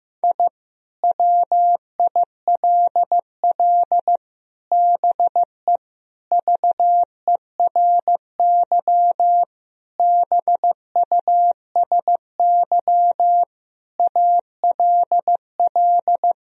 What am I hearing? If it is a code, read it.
Morse code, heard as a keyed tone, about 15 wpm, 700 Hz. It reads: I WILL BE VERY BUSY ALL
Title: Sentences from Top 1000 Words - Review Part 3 - 15wpm